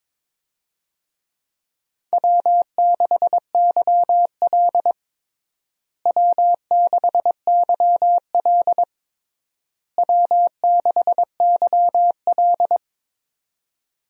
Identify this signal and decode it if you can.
Morse code, heard as a keyed tone, about 22 wpm, 700 Hz. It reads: W6YL W6YL W6YL
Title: USA Call Signs - 4 Characters - Repeated 3x - 22wpm